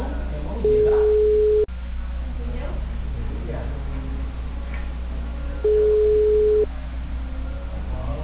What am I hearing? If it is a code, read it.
Ringback tone (Europe, 425 Hz), with speech.